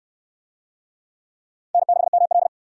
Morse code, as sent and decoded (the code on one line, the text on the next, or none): -. -.... -- .-..
N6ML